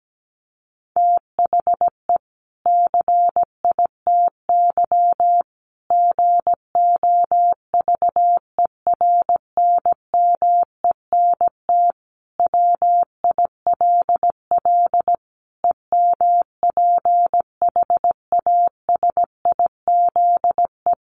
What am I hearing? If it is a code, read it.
Morse code, heard as a keyed tone, about 17 wpm, 700 Hz. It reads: THE CITY GOVERNMENT WILL EMPHASIZE
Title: Sentences from Top 1500 Words 17wpm